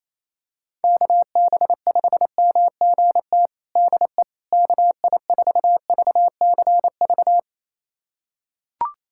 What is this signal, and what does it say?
Morse code, heard as a keyed tone, about 28 wpm, 700 Hz. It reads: KB5MGT DE KI4VCV
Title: USA Callsign DE Callsign - 6 Characters - 28wpm